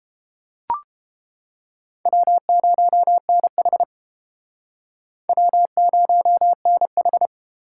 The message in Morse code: W0NH W0NH

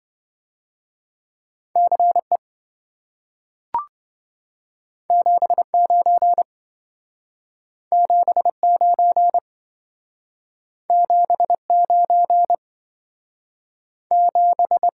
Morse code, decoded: CE 79 79 79 7

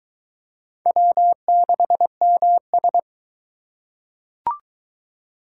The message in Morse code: W6MS

